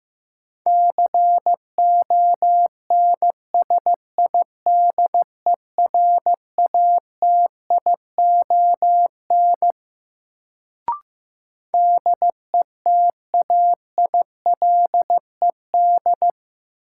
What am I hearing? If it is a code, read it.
Morse code, heard as a keyed tone, about 15 wpm, 700 Hz. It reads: CONSIDERATION DETAILED
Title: Sentences from Top 1500 Words - New Word Drill - 15wpm